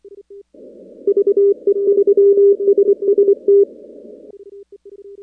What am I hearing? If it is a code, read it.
Morse code, heard as a keyed tone, about 24 wpm, 410 Hz. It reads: VE3SST
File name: contests / 2019-cw-wpx-kt5j-k5pi- -2330 0-15